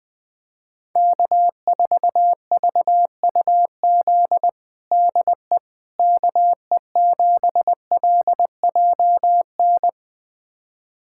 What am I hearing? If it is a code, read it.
Morse code, heard as a keyed tone, about 20 wpm, 700 Hz. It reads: K4VUZ DE KE7LJN
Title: USA Callsign DE Callsign 20wpm